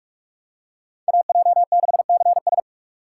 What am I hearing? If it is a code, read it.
Morse code, heard as a keyed tone, about 43 wpm, 700 Hz. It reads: AJ6CS